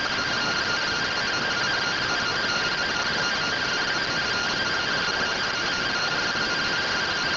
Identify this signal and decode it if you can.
Two-tone teleprinter signal, about 170 Hz shift.